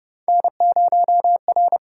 Morse code, heard as a keyed tone, about 31 wpm, 700 Hz. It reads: N0R
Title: USA Call Signs - 4 Characters - 6x Character Spacing - 30wpm